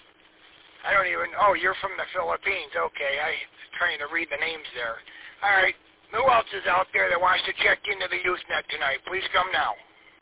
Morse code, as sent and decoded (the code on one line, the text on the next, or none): none